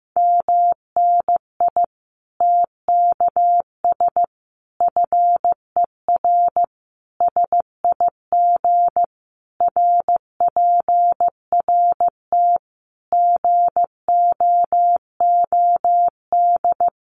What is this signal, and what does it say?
Morse code, heard as a keyed tone, about 15 wpm, 700 Hz. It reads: MNI TKS FER SIG RPRT GOOD
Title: Common QSO Phrases 15wpm